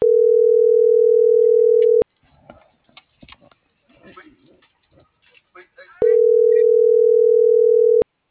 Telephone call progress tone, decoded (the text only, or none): ringback tone